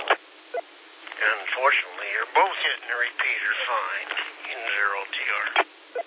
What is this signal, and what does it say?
Emergency Alert System (SAME) tones, heard as harsh buzzing bursts, with speech. It is heard from 1.1 to 4.9 s.